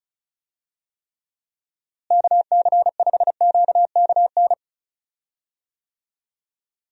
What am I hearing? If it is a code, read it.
Morse code, heard as a keyed tone, about 34 wpm, 700 Hz. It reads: KC5QKN
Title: USA Call Signs - 6 Characters - Repeated 3x - 35wpm